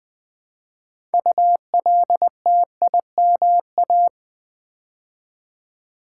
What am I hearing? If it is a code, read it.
Morse code, heard as a keyed tone, about 20 wpm, 700 Hz. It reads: ULTIMA
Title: Build A Word - 2 Letters At A Time - 8-Letter Words - 20wpm